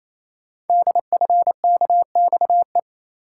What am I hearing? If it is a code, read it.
Morse code, heard as a keyed tone, about 27 wpm, 700 Hz. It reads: DFKXE
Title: Code Groups - 5 Letters - 28wpm